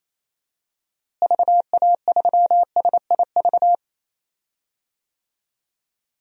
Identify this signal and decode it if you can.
Morse code, heard as a keyed tone, about 28 wpm, 700 Hz. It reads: VA3SIV
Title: DXCC-100 Call Signs - 6 Characters - Repeated 2x - 28wpm